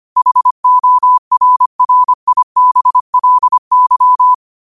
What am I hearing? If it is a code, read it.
Morse code, heard as a keyed tone, about 25 wpm, 1000 Hz. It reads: HORRIBLY